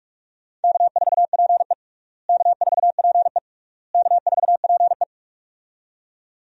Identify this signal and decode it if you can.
Morse code, heard as a keyed tone, about 42 wpm, 700 Hz. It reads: K4PE K4PE K4PE
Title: USA Call Signs - 4 Characters - Repeated 3x - 45wpm